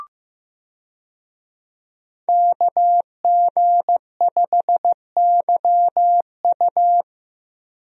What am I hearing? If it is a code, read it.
Morse code, heard as a keyed tone, about 15 wpm, 700 Hz. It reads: KG5YU
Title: USA Call Signs - 5 Characters - Repeated 2x - 15wpm